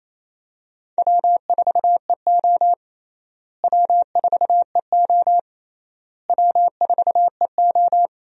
Morse code, heard as a keyed tone, about 28 wpm, 700 Hz. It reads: W4EO W4EO W4EO
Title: USA Call Signs - 4 Characters - Repeated 3x - 28wpm